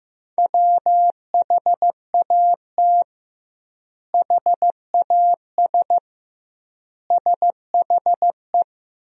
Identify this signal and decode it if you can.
Morse code, heard as a keyed tone, about 16 wpm, 700 Hz. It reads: WHAT HAS SHE
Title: 2x Word-Spacing - Sentences from Top 100 Words - 15wpm